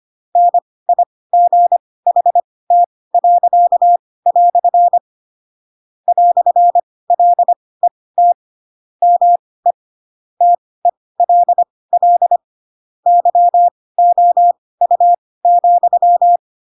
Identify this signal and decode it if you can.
Morse code, heard as a keyed tone, about 25 wpm, spaced as for 18 wpm, 700 Hz. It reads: NIGHT." "LET ME TELL YOU,
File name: Pinocchio / Pinocchio ch4 12 25 ik1hge